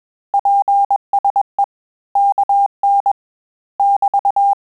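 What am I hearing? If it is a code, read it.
Morse code, heard as a keyed tone, about 21 wpm, 800 Hz. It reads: PSE KN <BT>
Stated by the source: Morse QSO practice at 20 words per minute, in Italian and English.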